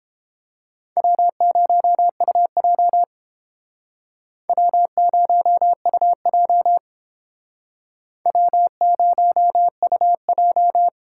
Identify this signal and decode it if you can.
Morse code, heard as a keyed tone, about 29 wpm, 700 Hz. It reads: W0UJ W0UJ W0UJ